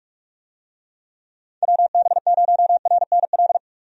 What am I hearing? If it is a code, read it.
Morse code, heard as a keyed tone, about 46 wpm, 700 Hz. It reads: WB0RNL
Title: USA Call Signs - 6 Characters - 45wpm